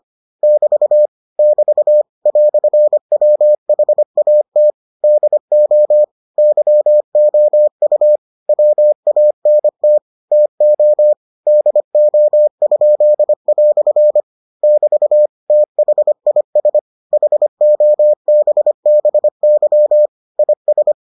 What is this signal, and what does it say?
Morse code, heard as a keyed tone, about 25 wpm, 600 Hz. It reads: <BT> <BT> "WHAT DO YOU WANT TO DO?" <BT> THIS HOBBY IS